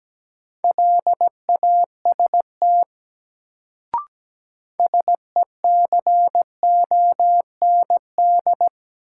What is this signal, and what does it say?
Morse code, heard as a keyed tone, about 17 wpm, 700 Hz. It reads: LAST SECOND
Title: Sets of 2 Words - New Word Drill - 17wpm